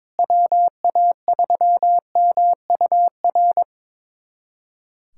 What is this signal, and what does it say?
Morse code, heard as a keyed tone, about 22 wpm, 700 Hz. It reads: WA3MUR
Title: USA Callsign DE Callsign - 6 Characters - 22wpm